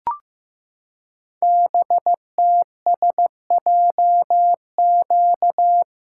Morse code, heard as a keyed tone, about 15 wpm, 700 Hz. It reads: BTSJQ
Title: Code Groups - 5 Letters - 15wpm